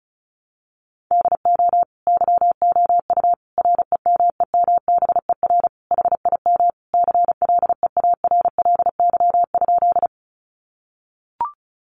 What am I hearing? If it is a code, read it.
Morse code, heard as a keyed tone, about 34 wpm, 700 Hz. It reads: DO YOU REMEMBER HIM CLEARLY?
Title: Sentences from Top 1300 Words 35wpm